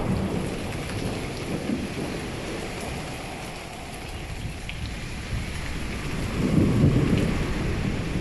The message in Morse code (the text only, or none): none